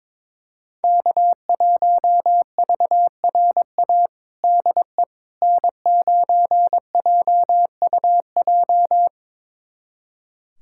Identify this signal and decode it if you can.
Morse code, heard as a keyed tone, about 21 wpm, 700 Hz. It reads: K1VRA DE N9JUJ
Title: USA Callsign DE Callsign - 5 Characters - 22wpm